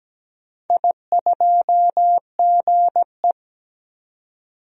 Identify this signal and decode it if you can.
Morse code, heard as a keyed tone, about 17 wpm, 700 Hz. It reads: I2GE
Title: DXCC-100 Call Signs - 4 Characters - 17wpm